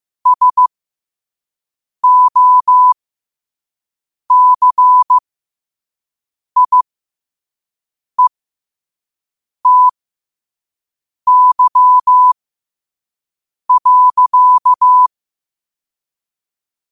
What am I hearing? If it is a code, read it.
Morse code, heard as a keyed tone, about 15 wpm, spaced as for 5 wpm, 1000 Hz. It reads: SOCIETY.
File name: qotdfiles / 5wpm